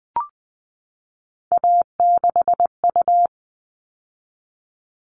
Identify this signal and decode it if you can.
Morse code, heard as a keyed tone, about 20 wpm, 700 Hz. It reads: A6U
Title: All Cut Numbers - 3-Digit 20wpm